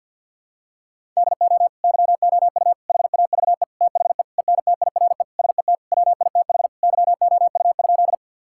Morse code, heard as a keyed tone, about 50 wpm, 700 Hz. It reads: DO YOU HAVE THE ENTIRE SET WITH YOU?
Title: Sentences from Top 1000 Words - Review Part 5 - 50wpm